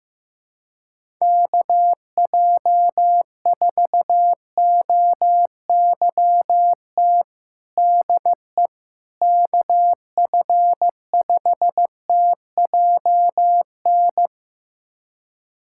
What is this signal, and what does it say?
Morse code, heard as a keyed tone, about 16 wpm, 700 Hz. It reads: KJ4OYT DE KF5TJN
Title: USA Callsign DE Callsign - 6 Characters - 15wpm